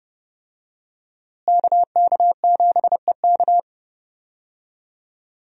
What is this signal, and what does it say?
Morse code, heard as a keyed tone, about 30 wpm, 700 Hz. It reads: KK7EK
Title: USA Call Signs - 5 Characters - Repeated 2x - 30wpm